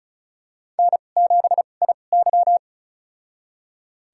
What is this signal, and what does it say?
Morse code, heard as a keyed tone, about 34 wpm, spaced as for 25 wpm, 700 Hz. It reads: N7IY
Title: USA Call Signs - 4 Characters - 2x Character Spacing - 35wpm